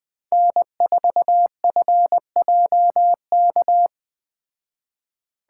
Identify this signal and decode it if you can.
Morse code, heard as a keyed tone, about 20 wpm, 700 Hz. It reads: N4FJK
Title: Speed Racing - USA 1x3 Call Signs 17wpm